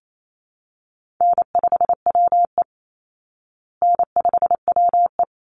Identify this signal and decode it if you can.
Morse code, heard as a keyed tone, about 28 wpm, 700 Hz. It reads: N5WE N5WE
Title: Easy Call Signs Repeated 3x - 28wpm